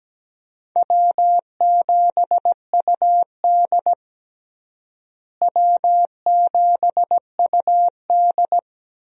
Morse code, heard as a keyed tone, about 17 wpm, 700 Hz. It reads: W7UD W7UD
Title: USA Call Signs - 4 Characters - Repeated 2x - 17wpm